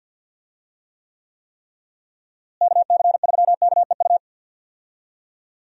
Morse code, heard as a keyed tone, about 46 wpm, 700 Hz. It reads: KC3KEU
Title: USA Call Signs - 6 Characters - Repeated 3x - 50wpm